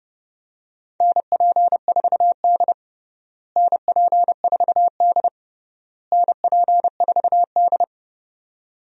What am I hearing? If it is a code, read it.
Morse code, heard as a keyed tone, about 32 wpm, 700 Hz. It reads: NP4D NP4D NP4D
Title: USA Call Signs - 4 Characters - Repeated 3x - 30wpm